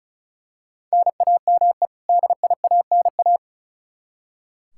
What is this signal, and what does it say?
Morse code, heard as a keyed tone, about 36 wpm, 700 Hz. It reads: NAME DIANA
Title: Practice QSO - Exchange 1 - Beginning - 35wpm